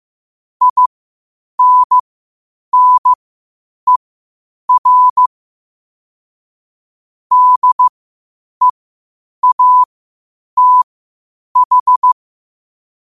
Morse code, heard as a keyed tone, about 15 wpm, spaced as for 8 wpm, 1000 Hz. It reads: INNER DEATH
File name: qotdfiles / 8wpm